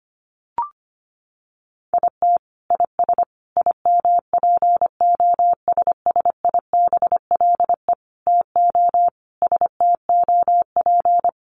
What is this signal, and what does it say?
Morse code, heard as a keyed tone, about 25 wpm, 700 Hz. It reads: IT IS IMPOSSIBLE TO STOP
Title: Sentences from Top 1100 Words 25wpm